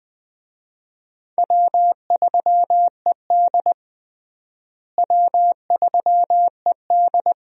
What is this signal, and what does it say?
Morse code, heard as a keyed tone, about 21 wpm, 700 Hz. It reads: W3ED W3ED
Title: USA Call Signs - 4 Characters - Repeated 2x - 20wpm